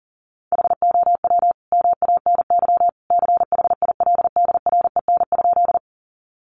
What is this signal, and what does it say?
Morse code, heard as a keyed tone, about 40 wpm, 700 Hz. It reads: HOW MANY CHILDREN?
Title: Sentences from Top 300 Words 40wpm